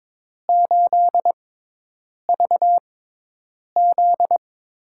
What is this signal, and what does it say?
Morse code, heard as a keyed tone, about 22 wpm, spaced as for 8 wpm, 700 Hz. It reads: 8VZ
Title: USA Call Signs - 4 Characters - 6x Character Spacing - 22wpm